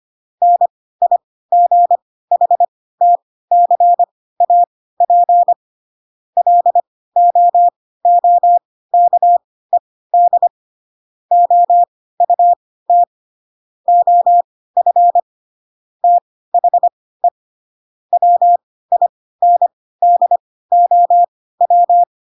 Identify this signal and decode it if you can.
Morse code, heard as a keyed tone, about 25 wpm, spaced as for 16 wpm, 700 Hz. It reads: NIGHTCAP LOOKED OUT OF THE WINDOW
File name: Pinocchio / Pinocchio ch7 10 25 ik1hge